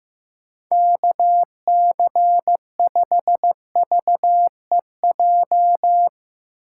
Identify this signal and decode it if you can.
Morse code, heard as a keyed tone, about 15 wpm, 700 Hz. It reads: KC5VEJ
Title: USA Call Signs - 6 Characters - 6x Character Spacing - 15wpm